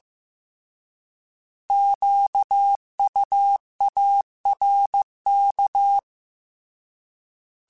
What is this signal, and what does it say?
Morse code, heard as a keyed tone, about 15 wpm, 800 Hz. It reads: QUARK
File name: Alphabetical / Alphabetical mp3s / Qs @13